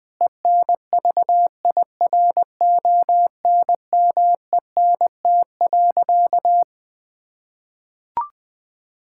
Morse code, heard as a keyed tone, about 20 wpm, 700 Hz. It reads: ENVIRONMENT.
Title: Sentences from Top 1000 Words - Review Part 5 - 20wpm